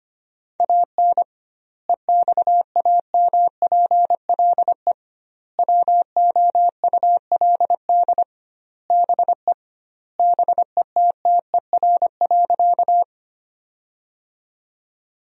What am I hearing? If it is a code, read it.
Morse code, heard as a keyed tone, about 25 wpm, 700 Hz. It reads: AN EXAMPLE WOULD BE BETTER.
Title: 2x Word-Spacing - Sentences from Top 400 Words - 25wpm